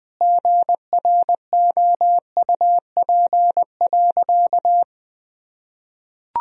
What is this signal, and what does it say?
Morse code, heard as a keyed tone, about 20 wpm, 700 Hz. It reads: GROUP.
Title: Sentences from Top 1000 Words - Review Part 1 - 20wpm